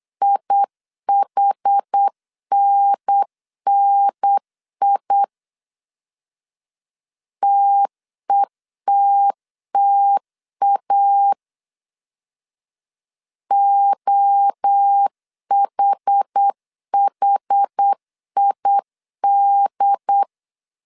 Morse code, heard as a keyed tone, about 8 wpm, 800 Hz. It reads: IHNNI TETTA OHHID
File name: CW / DET / 1 3 1500C S B 8wpm 4141